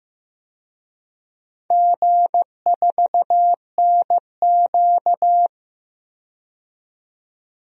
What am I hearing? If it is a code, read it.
Morse code, heard as a keyed tone, about 15 wpm, 700 Hz. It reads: G4NQ
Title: DXCC-100 Call Signs - 4 Characters - 15wpm